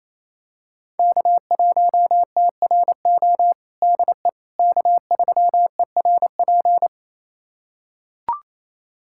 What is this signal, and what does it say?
Morse code, heard as a keyed tone, about 28 wpm, 700 Hz. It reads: K1TRO DE K3ERP
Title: USA Callsign DE Callsign 28wpm